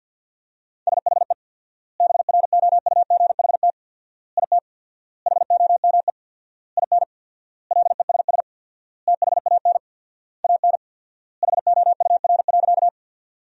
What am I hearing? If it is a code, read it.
Morse code, heard as a keyed tone, about 47 wpm, 700 Hz. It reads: SHE BROUGHT IT HOME IN LESS THAN AN HOUR.
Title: 4x Word-Spacing - Sentences from Top 500 Words - 50wpm